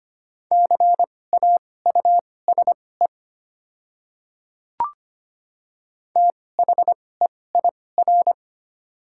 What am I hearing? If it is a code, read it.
Morse code, heard as a keyed tone, about 25 wpm, spaced as for 18 wpm, 700 Hz. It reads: CAUSE THEIR
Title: 2x Character Spacing - Top 300 Words - 25wpm